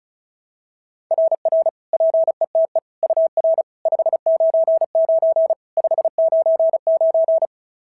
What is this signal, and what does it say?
Morse code, heard as a keyed tone, about 35 wpm, 690 Hz. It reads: RR PETE UR 599 599
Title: SOTA QSO Practice 35wpm